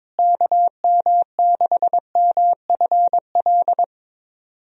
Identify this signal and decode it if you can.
Morse code, heard as a keyed tone, about 22 wpm, 700 Hz. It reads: KM6MFL
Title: USA Call Signs - 6 Characters - Repeated 2x - 22wpm